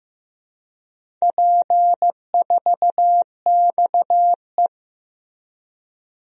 Morse code, heard as a keyed tone, about 15 wpm, 700 Hz. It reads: P4XE
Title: DXCC-100 Call Signs - 4 Characters - Repeated 3x - 15wpm